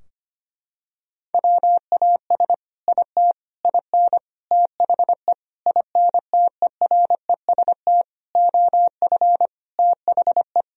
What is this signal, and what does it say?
Morse code, heard as a keyed tone, about 25 wpm, 700 Hz. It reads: WAS IT IN THE INTEREST OF THE
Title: Sentences from Top 1000 Words - Review Part 4 - 25wpm